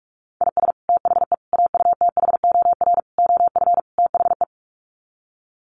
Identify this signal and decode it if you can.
Morse code, heard as a keyed tone, about 44 wpm, 700 Hz. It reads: IS THE AUTHOR OF THE
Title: Sets of 5 Words Encore 45wpm